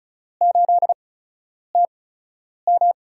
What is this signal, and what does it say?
Morse code, heard as a keyed tone, about 36 wpm, 700 Hz. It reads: 8 T M
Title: USA Call Signs - 4 Characters - 8x Character Spacing - 35wpm